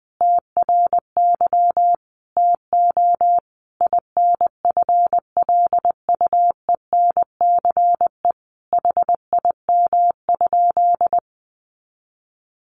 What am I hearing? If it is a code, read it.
Morse code, heard as a keyed tone, about 20 wpm, 700 Hz. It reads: TRY TO INFLUENCE HIM?
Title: Sentences from Top 1100 Words 20wpm